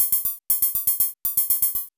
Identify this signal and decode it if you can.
There is no code.